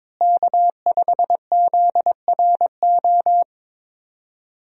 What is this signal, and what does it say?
Morse code, heard as a keyed tone, about 22 wpm, 700 Hz. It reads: K5ZRO
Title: USA Call Signs - 5 Characters - Repeated 3x - 22wpm